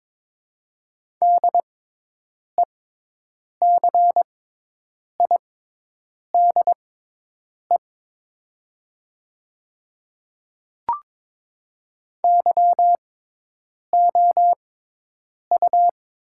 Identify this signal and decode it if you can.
Morse code, heard as a keyed tone, about 22 wpm, spaced as for 8 wpm, 700 Hz. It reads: DECIDE YOU
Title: 6x Character Spacing - Top 500 Words - 22wpm